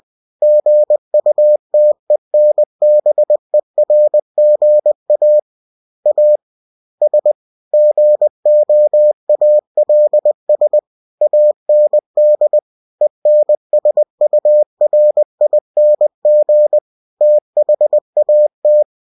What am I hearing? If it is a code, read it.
Morse code, heard as a keyed tone, about 20 wpm, 600 Hz. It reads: GUTENBERGA AS GOALS AND ENSURING THAT
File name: cw-books / winnie-the-pooh / winnie-20wpm-page-0120